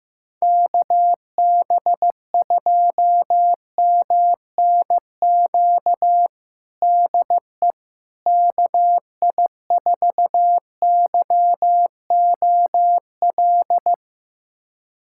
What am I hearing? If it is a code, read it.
Morse code, heard as a keyed tone, about 15 wpm, 700 Hz. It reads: KB2MNQ DE KI4YOL
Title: USA Callsign DE Callsign - 6 Characters - 15wpm